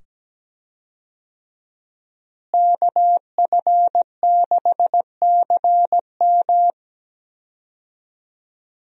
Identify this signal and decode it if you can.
Morse code, heard as a keyed tone, about 17 wpm, 700 Hz. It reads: KF6CM
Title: USA Call Signs - 5 Characters - Repeated 2x - 17wpm